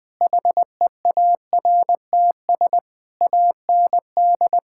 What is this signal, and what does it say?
Morse code, heard as a keyed tone, about 20 wpm, 700 Hz. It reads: HEARTS AND
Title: Sentences with 8-Word Phrases - Spoken Beginning - 20wpm